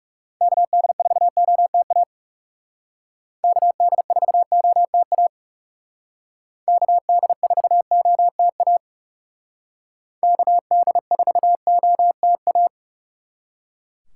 Morse code, heard as a keyed tone, about 37 wpm, 700 Hz. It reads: KD4OTA KD4OTA KD4OTA KD4OTA
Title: Speed Racing - USA 2x3 Call Signs 30wpm